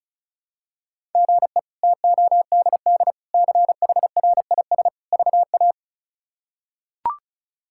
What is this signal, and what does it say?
Morse code, heard as a keyed tone, about 36 wpm, 700 Hz. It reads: GE TODD CHRIS VA
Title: K1USN SST QSO Practice - 35wpm